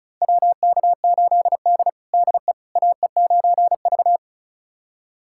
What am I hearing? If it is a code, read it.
Morse code, heard as a keyed tone, about 36 wpm, 700 Hz. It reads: WK8D DE AE9V
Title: USA Callsign DE Callsign - 4 Characters - 35wpm